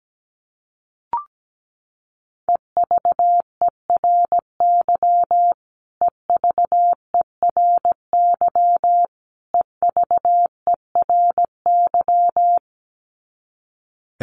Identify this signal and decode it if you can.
Morse code, heard as a keyed tone, about 17 wpm, 700 Hz. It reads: EVERY EVERY EVERY
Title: Top 300 Words Repeated 17wpm